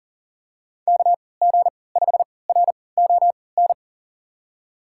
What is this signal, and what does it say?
Morse code, heard as a keyed tone, about 39 wpm, spaced as for 23 wpm, 700 Hz. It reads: KG5RON
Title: USA Call Signs - 6 Characters - 3x Character Spacing - 40wpm